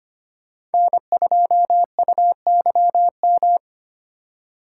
Morse code, heard as a keyed tone, about 31 wpm, 700 Hz. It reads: N2UYM
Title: Speed Racing - USA 1x3 Call Signs 25wpm